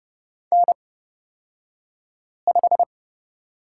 Morse code, heard as a keyed tone, about 29 wpm, spaced as for 5 wpm, 690 Hz. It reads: N5
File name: Lesson 36d - BT - Two Characters 30wpm